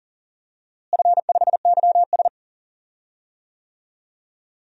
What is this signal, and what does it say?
Morse code, heard as a keyed tone, about 37 wpm, 700 Hz. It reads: F5YS
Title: DXCC-100 Call Signs - 4 Characters - Repeated 2x - 40wpm